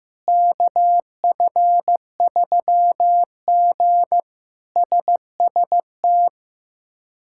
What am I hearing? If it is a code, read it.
Morse code, heard as a keyed tone, about 15 wpm, 700 Hz. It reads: KF3G SST
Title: K1USN SST QSO Practice - 15wpm